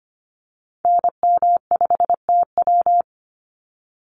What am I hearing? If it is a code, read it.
Morse code, heard as a keyed tone, about 25 wpm, 700 Hz. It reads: NM5TW
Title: NA SOTA Activator Callsigns 25wpm